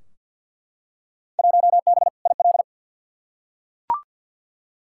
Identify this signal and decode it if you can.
Morse code, heard as a keyed tone, about 52 wpm, 700 Hz. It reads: 1B IL
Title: Field Day Exchanges - 50wpm